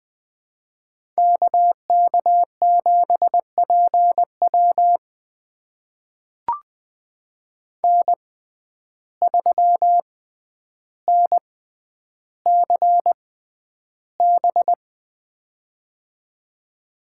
Morse code, heard as a keyed tone, about 20 wpm, spaced as for 10 wpm, 700 Hz. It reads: KK7PW N3NCB
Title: USA Call Signs - 5 Characters - 6x Character Spacing - 20wpm